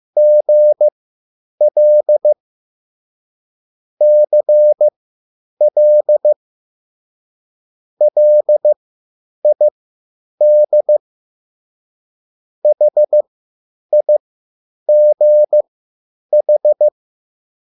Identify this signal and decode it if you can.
Morse code, heard as a keyed tone, about 15 wpm, spaced as for 9 wpm, 600 Hz. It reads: GL CL LID HIGH